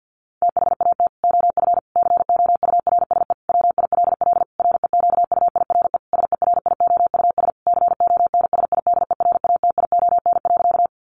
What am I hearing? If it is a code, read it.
Morse code, heard as a keyed tone, about 50 wpm, 700 Hz. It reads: THAT OF COURSE WILL REQUIRE SERIOUS CONSIDERATION.
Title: Sentences from Top 1500 Words 50wpm